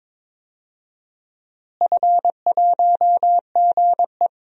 Morse code, heard as a keyed tone, about 22 wpm, 700 Hz. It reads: F1GE
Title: DXCC-100 Call Signs - 4 Characters - Repeated 2x - 22wpm